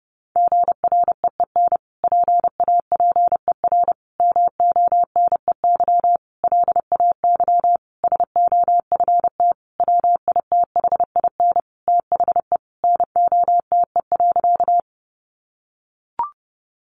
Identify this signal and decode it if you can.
Morse code, heard as a keyed tone, about 30 wpm, 700 Hz. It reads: GREEN PAPER MONEY LAY SOFT WITHIN THE NOTE.
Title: Sentences from Top 1500 Words 30wpm